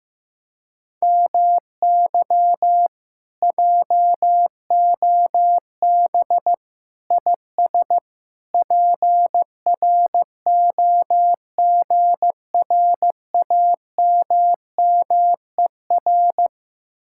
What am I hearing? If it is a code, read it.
Morse code, heard as a keyed tone, about 15 wpm, 700 Hz. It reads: MY JOB IS PROGRAMMER
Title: Speed-Racing Common QSO Phrases Encore - 15wpm